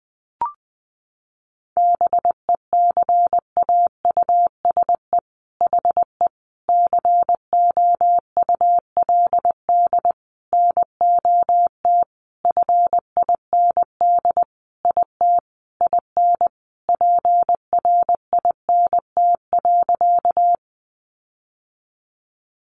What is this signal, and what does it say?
Morse code, heard as a keyed tone, about 19 wpm, 700 Hz. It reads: BECAUSE HE COULD NOT FIND IT IN PRINT.
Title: Sentences from Top 1400 Words 20wpm